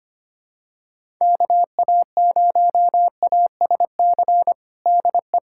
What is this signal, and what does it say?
Morse code, heard as a keyed tone, about 25 wpm, 700 Hz. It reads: KA0ASC DE
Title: USA Callsign DE Callsign - 6 Characters - 25wpm